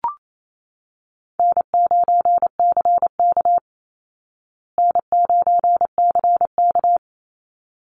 Morse code, heard as a keyed tone, about 28 wpm, 700 Hz. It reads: N9CK N9CK
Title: Easy Call Signs Repeated 2x - 28wpm